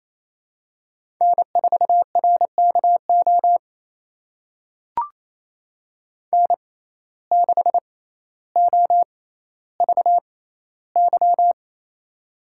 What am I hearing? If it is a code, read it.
Morse code, heard as a keyed tone, about 28 wpm, spaced as for 14 wpm, 700 Hz. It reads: N4RKO N6OVY
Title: USA Call Signs - 5 Characters - 6x Character Spacing - 28wpm